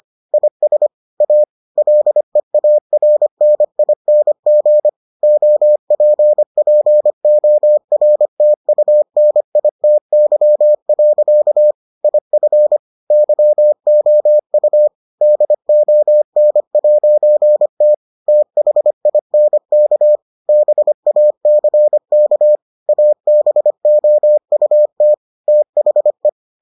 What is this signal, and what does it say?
Morse code, heard as a keyed tone, about 25 wpm, 600 Hz. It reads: IS A LEARNING OPPORTUNITY. IF YOU DON'T THINK BACK ABOUT THE